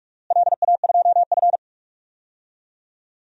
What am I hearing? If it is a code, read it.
Morse code, heard as a keyed tone, about 46 wpm, 700 Hz. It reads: LA2F
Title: DXCC-100 Call Signs - 4 Characters - Repeated 2x - 45wpm